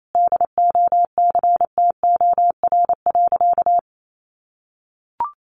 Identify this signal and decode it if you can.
Morse code, heard as a keyed tone, about 28 wpm, 700 Hz. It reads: DOCTOR.
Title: Sentences from Top 1500 Words 28wpm